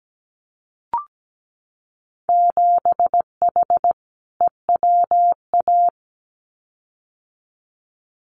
Morse code, heard as a keyed tone, about 17 wpm, 700 Hz. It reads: 7H EWA
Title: Winter Field Day Exchanges Encore 17wpm